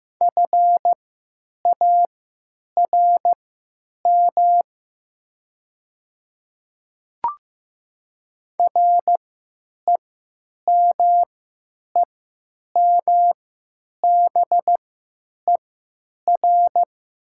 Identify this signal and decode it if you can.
Morse code, heard as a keyed tone, about 15 wpm, spaced as for 9 wpm, 700 Hz. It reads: FARM REMEMBER
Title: 3x Character Spacing - Top 500 Words - 15wpm